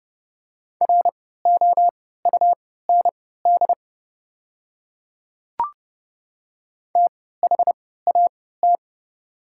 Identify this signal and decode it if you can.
Morse code, heard as a keyed tone, about 30 wpm, spaced as for 17 wpm, 690 Hz. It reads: ROUND THAT
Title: 3x Character Spacing - Top 200 Words - 30wpm